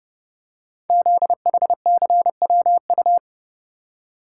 Morse code, heard as a keyed tone, about 30 wpm, 700 Hz. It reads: ZHCWU
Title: Code Groups - 5 Letters - 30wpm